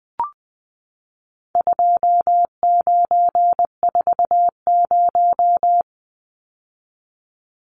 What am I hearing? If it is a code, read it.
Morse code, heard as a keyed tone, about 20 wpm, 700 Hz. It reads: 2940
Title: Numbers - 4-Digit 20wpm